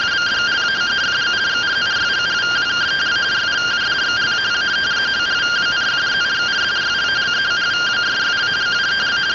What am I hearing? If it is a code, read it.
Two-tone teleprinter signal, about 170 Hz shift.